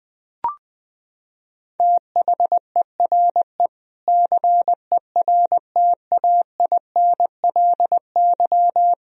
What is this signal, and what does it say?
Morse code, heard as a keyed tone, about 20 wpm, 700 Hz. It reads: THERE CERTAINLY
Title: Sentences from Top 1000 Words - Review Part 3 - 20wpm